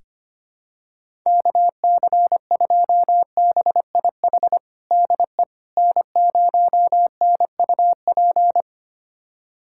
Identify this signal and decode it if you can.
Morse code, heard as a keyed tone, about 25 wpm, 700 Hz. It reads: KC2BIH DE N0NUP
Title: USA Callsign DE Callsign 25wpm